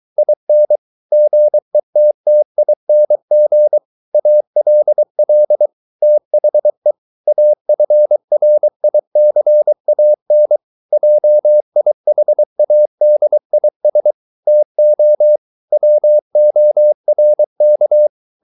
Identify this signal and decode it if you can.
Morse code, heard as a keyed tone, about 23 wpm, 600 Hz. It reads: IN GETTING ALL THE AFRICAN JIHADIS TO WORK